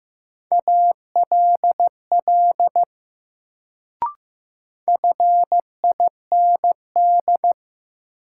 Morse code, heard as a keyed tone, about 15 wpm, 700 Hz. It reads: ALL FIND
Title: Sentences from Top 100 Words - New Word Drill - 15wpm